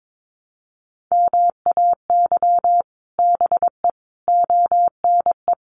Morse code, heard as a keyed tone, about 22 wpm, 700 Hz. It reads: MAY BE ONE
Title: Sets of 5 Words Encore2 22wpm